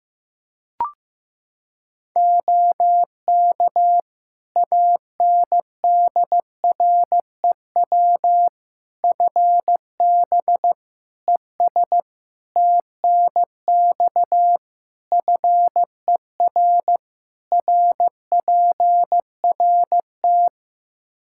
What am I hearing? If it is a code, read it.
Morse code, heard as a keyed tone, about 16 wpm, 700 Hz. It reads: OK ANDREW FB ES TNX FER RPRT
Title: Practice QSO - Exchange 2 - Info - 15wpm